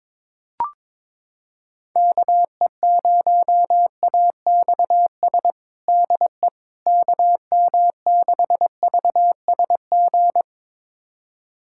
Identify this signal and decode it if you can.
Morse code, heard as a keyed tone, about 22 wpm, 700 Hz. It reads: KE0AXS DE KM6VSG
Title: USA Callsign DE Callsign 22wpm